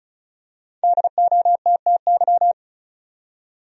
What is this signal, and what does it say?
Morse code, heard as a keyed tone, about 36 wpm, 700 Hz. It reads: DOTTY
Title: USA Common Names - 35wpm